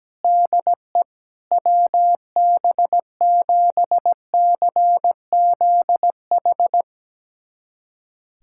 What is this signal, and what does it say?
Morse code, heard as a keyed tone, about 17 wpm, 700 Hz. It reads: DE WB7CZH
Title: USA Callsign DE Callsign 17wpm